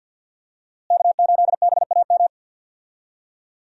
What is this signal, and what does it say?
Morse code, heard as a keyed tone, about 48 wpm, 700 Hz. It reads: K8BAM
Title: USA Call Signs - 5 Characters - Repeated 2x - 50wpm